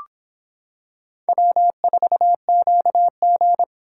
Morse code, heard as a keyed tone, about 26 wpm, 700 Hz. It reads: W4QG